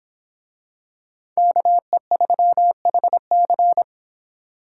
Morse code, heard as a keyed tone, about 25 wpm, 700 Hz. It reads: KE3HC